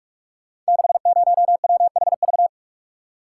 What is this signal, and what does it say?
Morse code, heard as a keyed tone, about 44 wpm, 700 Hz. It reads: 60WHV
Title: Code Groups - 5 Letter:Numbers - 45wpm